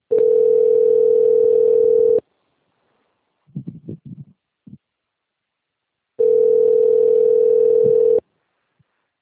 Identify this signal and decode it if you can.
Ringback tone (North America), 440 + 480 Hz.